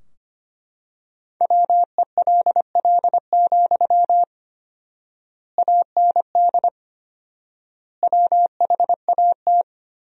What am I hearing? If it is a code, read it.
Morse code, heard as a keyed tone, about 25 wpm, 700 Hz. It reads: WELL, AND WHAT
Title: 4x Word-Spacing - Sentences from Top 200 Words - 25wpm